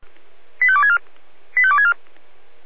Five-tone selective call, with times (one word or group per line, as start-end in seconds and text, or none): 0.6-1.0 85264
1.6-1.9 85264